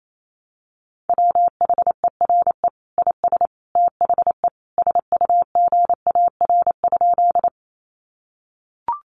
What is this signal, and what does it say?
Morse code, heard as a keyed tone, about 28 wpm, 700 Hz. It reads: WHERE IS THE SUGAR?
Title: Sentences from Top 1200 Words 28wpm